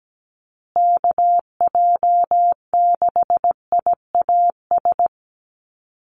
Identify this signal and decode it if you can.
Morse code, heard as a keyed tone, about 17 wpm, 700 Hz. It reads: KJ6IAS
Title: USA Call Signs Repeated 2x 17wpm